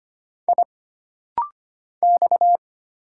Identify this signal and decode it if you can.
Morse code, heard as a keyed tone, about 25 wpm, 700 Hz. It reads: I X
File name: Single Character (Letter:Number:Punctuation:Prosign) - Rapid-Fire - 25wpm